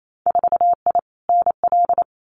Morse code, heard as a keyed tone, about 28 wpm, 700 Hz. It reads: 4I NL
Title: Winter Field Day Exchanges Encore 28wpm